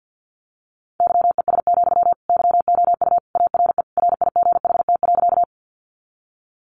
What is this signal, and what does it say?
Morse code, heard as a keyed tone, about 50 wpm, 700 Hz. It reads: YES, YOU ARE RIGHT.
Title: Sentences from Top 500 Words 50wpm